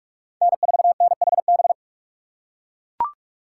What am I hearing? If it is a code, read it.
Morse code, heard as a keyed tone, about 42 wpm, 690 Hz. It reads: N4NHB
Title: USA Call Signs - 5 Characters - 6x Character Spacing - 45wpm